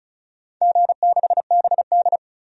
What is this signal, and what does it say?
Morse code, heard as a keyed tone, about 36 wpm, 700 Hz. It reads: G6BD
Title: DXCC-100 Call Signs - 4 Characters - 35wpm